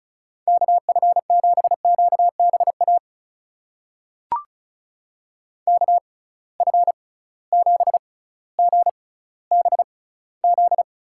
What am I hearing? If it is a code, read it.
Morse code, heard as a keyed tone, about 34 wpm, spaced as for 18 wpm, 700 Hz. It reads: KF7QBA KF7GBZ
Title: USA Call Signs - 6 Characters - 6x Character Spacing - 35wpm